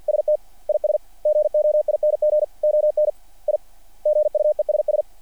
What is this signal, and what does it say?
Morse code, heard as a keyed tone, about 48 wpm, 600 Hz. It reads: ST IS GOING ON I GUESS
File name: archived / CRSnet